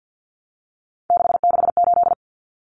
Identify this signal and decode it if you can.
Morse code, heard as a keyed tone, about 50 wpm, 700 Hz. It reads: 668
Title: Numbers - 3-Digit 50wpm